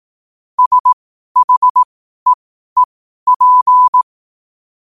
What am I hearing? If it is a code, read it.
Morse code, heard as a keyed tone, about 18 wpm, spaced as for 12 wpm, 1000 Hz. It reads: SHEEP
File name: qotdfiles / 12wpm